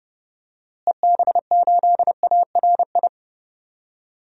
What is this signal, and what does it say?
Morse code, heard as a keyed tone, about 32 wpm, 700 Hz. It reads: EB8ARI
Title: DXCC-100 Call Signs - 6 Characters - Repeated 3x - 30wpm